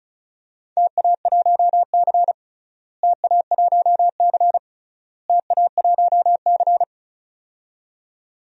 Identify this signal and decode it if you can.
Morse code, heard as a keyed tone, about 36 wpm, 700 Hz. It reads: TA1C TA1C TA1C
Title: DXCC-100 Call Signs - 4 Characters - Repeated 3x - 35wpm